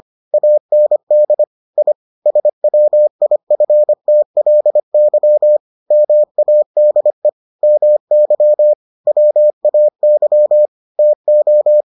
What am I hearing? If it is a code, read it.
Morse code, heard as a keyed tone, about 25 wpm, 600 Hz. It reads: AND I SWIFTLY MADE MY WAY TO